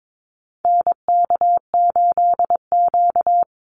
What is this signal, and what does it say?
Morse code, heard as a keyed tone, about 22 wpm, 700 Hz. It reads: NK8Q
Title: NA SOTA Activator Callsigns 22wpm